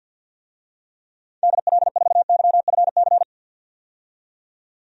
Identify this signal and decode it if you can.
Morse code, heard as a keyed tone, about 52 wpm, 700 Hz. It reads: DL4YFC
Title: DXCC-100 Call Signs - 6 Characters - 50wpm